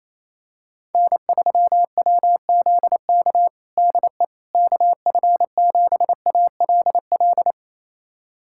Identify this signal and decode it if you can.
Morse code, heard as a keyed tone, about 28 wpm, 700 Hz. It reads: N3WZK DE KF7ALL
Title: USA Callsign DE Callsign 28wpm